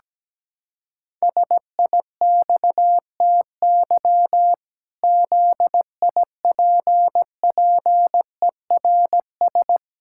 Morse code, heard as a keyed tone, about 17 wpm, 700 Hz. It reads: SIXTY ZIPPERS
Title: Pangram Challenge v2 - 17wpm